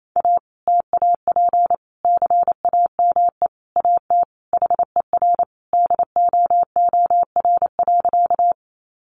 Morse code, heard as a keyed tone, about 28 wpm, 700 Hz. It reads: A TAP CAME AT HER DOOR.
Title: Sentences from Top 1400 Words 28wpm